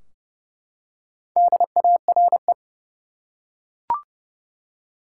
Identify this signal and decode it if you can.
Morse code, heard as a keyed tone, about 30 wpm, 700 Hz. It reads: DARE
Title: Four Letter Words - 30wpm